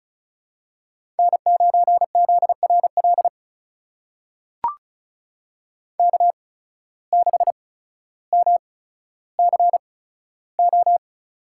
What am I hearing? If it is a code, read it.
Morse code, heard as a keyed tone, about 36 wpm, spaced as for 15 wpm, 700 Hz. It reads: N9ZRL K6MCO
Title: USA Call Signs - 5 Characters - 8x Character Spacing - 35wpm